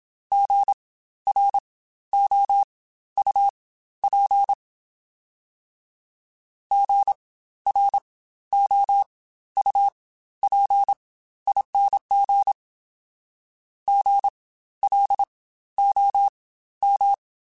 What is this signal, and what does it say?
Morse code, heard as a keyed tone, about 27 wpm, spaced as for 13 wpm, 800 Hz. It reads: GROUP GROUPING GLOM
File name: Spaced Alphas / Spaced Alphas mp3s / Gs sp @25